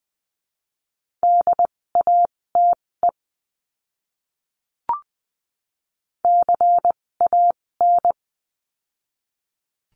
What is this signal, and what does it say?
Morse code, heard as a keyed tone, about 20 wpm, spaced as for 16 wpm, 700 Hz. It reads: DATE CAN
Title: 256 Lesson 42a - Words 12-20wpm